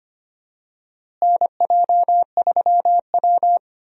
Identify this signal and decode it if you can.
Morse code, heard as a keyed tone, about 25 wpm, 700 Hz. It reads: NJ3W
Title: USA Call Signs - 4 Characters - Repeated 2x - 25wpm